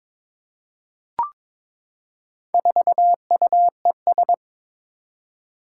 Morse code, heard as a keyed tone, about 22 wpm, 700 Hz. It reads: 4UES
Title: Four Challenging Letter-Number Combinations 22wpm